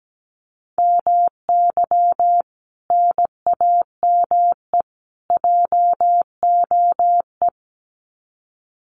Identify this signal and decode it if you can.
Morse code, heard as a keyed tone, about 17 wpm, 700 Hz. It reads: MY NAME JOE
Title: Common QSO Phrases 17wpm